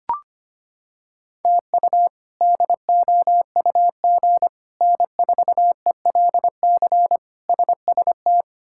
Morse code, heard as a keyed tone, about 25 wpm, 700 Hz. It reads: TU DOUG N4ELC SST
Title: K1USN SST QSO Practice - 25wpm